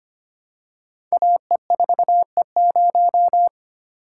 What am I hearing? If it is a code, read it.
Morse code, heard as a keyed tone, about 25 wpm, 690 Hz. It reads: AE4E0
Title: Code Groups - 5 Letter:Numbers - 25wpm